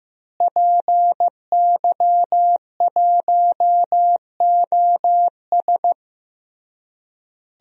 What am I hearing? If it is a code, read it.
Morse code, heard as a keyed tone, about 15 wpm, 700 Hz. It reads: PY1OS
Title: DXCC-100 Call Signs - 5 Characters - Repeated 3x - 15wpm